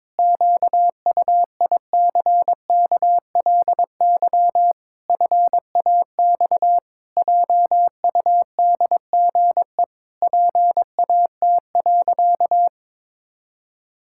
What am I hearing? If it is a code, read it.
Morse code, heard as a keyed tone, about 22 wpm, 700 Hz. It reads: QUICKLY FAX JUDGE PAT.
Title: Pangram Challenge v2 - 22wpm